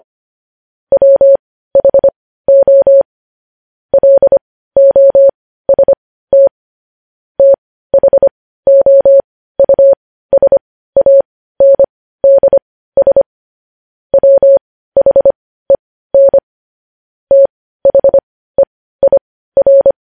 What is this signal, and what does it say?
Morse code, heard as a keyed tone, about 25 wpm, spaced as for 15 wpm, 550 Hz. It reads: WHO LOST THOUSANDS WHEN THEIR
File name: podcast / News headlines-1x-25x15